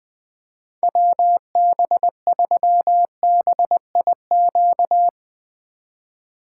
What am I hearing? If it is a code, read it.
Morse code, heard as a keyed tone, about 20 wpm, 700 Hz. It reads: WB3BIQ
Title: K1USN SST QSO Practice - 20wpm